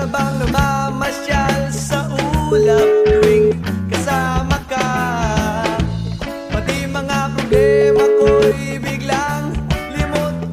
Ringback tone (Europe, 425 Hz), with noise.